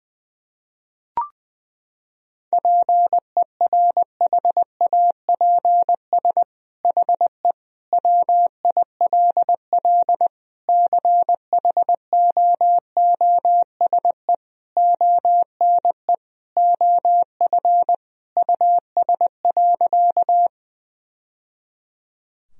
Morse code, heard as a keyed tone, about 20 wpm, 700 Hz. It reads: PERHAPS HE WILL CHOOSE ONE OF US.
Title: Sentences from Top 1000 Words - Review Part 4 - 20wpm